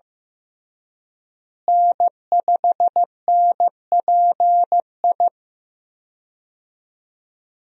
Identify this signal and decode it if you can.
Morse code, heard as a keyed tone, about 16 wpm, 700 Hz. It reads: N5NPI
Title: USA Call Signs - 5 Characters - Repeated 2x - 15wpm